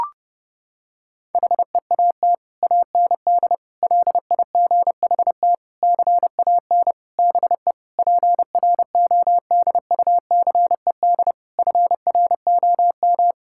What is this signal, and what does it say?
Morse code, heard as a keyed tone, about 32 wpm, 700 Hz. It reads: HEAT AND LIGHT CAN BE PRODUCED FROM
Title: Sentences from Top 1000 Words - Review Part 6 - 30wpm